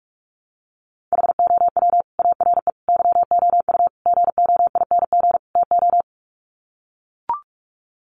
Morse code, heard as a keyed tone, about 44 wpm, 700 Hz. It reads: HOW ARE YOU GOING TO
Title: Sets of 5 Words 45wpm